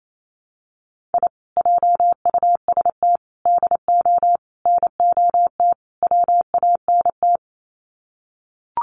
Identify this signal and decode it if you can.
Morse code, heard as a keyed tone, about 28 wpm, 700 Hz. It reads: I JUST DO NOT WANT
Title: Sets of 5 Words Encore2 28wpm